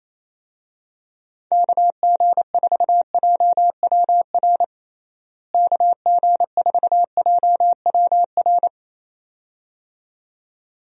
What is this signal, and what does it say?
Morse code, heard as a keyed tone, about 28 wpm, 700 Hz. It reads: KG4JWR KG4JWR
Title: USA Call Signs - 6 Characters - Repeated 2x - 28wpm